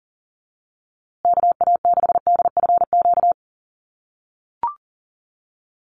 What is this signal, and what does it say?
Morse code, heard as a keyed tone, about 40 wpm, 700 Hz. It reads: KA6DFQ
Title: USA Call Signs Repeated 2x 40wpm